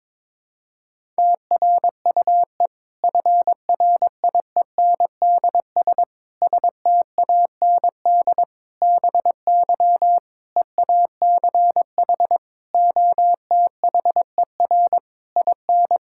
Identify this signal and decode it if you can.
Morse code, heard as a keyed tone, about 22 wpm, 700 Hz. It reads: TRUE FRIENDS STAND BY EACH OTHER IN
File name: Difficult Sentences with Hints - 22wpm